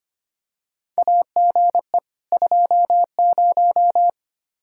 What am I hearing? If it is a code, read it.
Morse code, heard as a keyed tone, about 25 wpm, 700 Hz. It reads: AGE 20
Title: Practice QSO - Exchange 3 - More Info - 25wpm